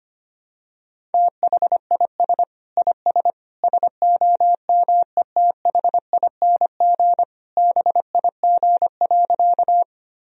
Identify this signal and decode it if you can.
Morse code, heard as a keyed tone, about 25 wpm, 700 Hz. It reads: THIS IS SOMETHING BIG.
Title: Sentences from Top 1000 Words - Review Part 1 - 25wpm